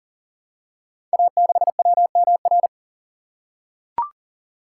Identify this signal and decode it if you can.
Morse code, heard as a keyed tone, about 43 wpm, 690 Hz. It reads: A6WMR
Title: Code Groups - 5 Letter:Numbers - 40wpm